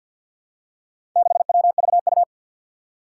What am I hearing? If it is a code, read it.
Morse code, heard as a keyed tone, about 49 wpm, 700 Hz. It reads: 6WVU
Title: Four Challenging Letter-Number Combinations 50wpm